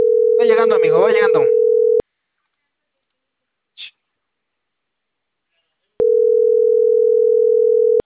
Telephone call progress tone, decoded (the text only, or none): ringback tone